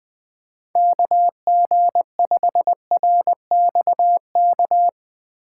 Morse code, heard as a keyed tone, about 20 wpm, 700 Hz. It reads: KG5RXK
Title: USA Call Signs - 6 Characters - 4x Character Spacing - 20wpm